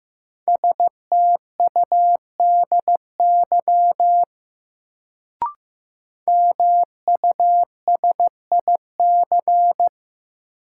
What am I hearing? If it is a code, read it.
Morse code, heard as a keyed tone, about 15 wpm, 700 Hz. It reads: STUDY MUSIC
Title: Sentences from Top 300 Words - New Word Drill - 15wpm